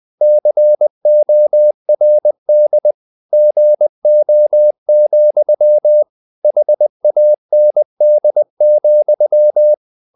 Morse code, heard as a keyed tone, about 20 wpm, 600 Hz. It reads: CORD GO, HAND,
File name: the old man and the sea / the old man and the sea part 1 20wpm